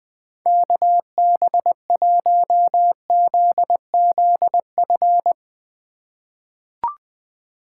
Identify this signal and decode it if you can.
Morse code, heard as a keyed tone, about 20 wpm, 700 Hz. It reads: KB1ZZF
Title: USA Call Signs - 6 Characters - 3x Character Spacing - 20wpm